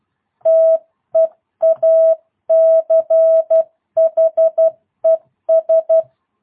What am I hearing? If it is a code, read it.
Morse code, heard as a keyed tone, about 12 wpm, 650 Hz. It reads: TEACHES
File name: jh8cbh / 2022cw